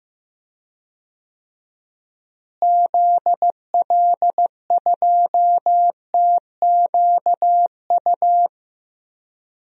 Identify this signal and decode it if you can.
Morse code, heard as a keyed tone, about 15 wpm, 700 Hz. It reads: ZL2TQU
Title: DXCC-100 Call Signs - 6 Characters - Repeated 2x - 15wpm